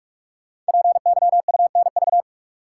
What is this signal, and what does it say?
Morse code, heard as a keyed tone, about 44 wpm, 700 Hz. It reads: PYUNV
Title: Code Groups - 5 Letters - 45wpm